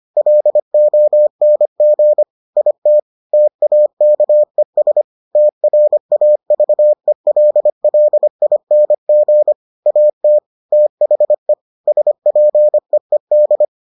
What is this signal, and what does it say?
Morse code, heard as a keyed tone, about 25 wpm, 600 Hz. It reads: LONG IT TAKES TRAVELLING AT THE SPEED